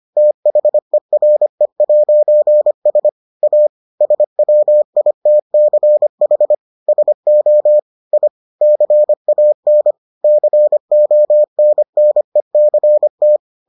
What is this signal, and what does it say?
Morse code, heard as a keyed tone, about 25 wpm, 600 Hz. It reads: THERE'S A SWITCH SO I CAN CONNECT